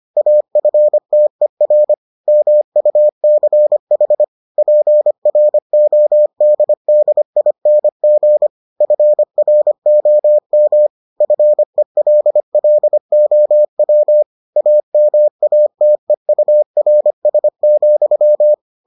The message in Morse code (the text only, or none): AFTER MUCH PRODDING FROM FELLOW AMATEURS,